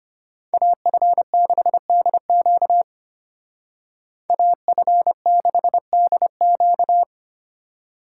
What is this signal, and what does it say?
Morse code, heard as a keyed tone, about 30 wpm, 700 Hz. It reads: AF6DQ AF6DQ
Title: Speed Racing - USA 2x2 Call Signs 25wpm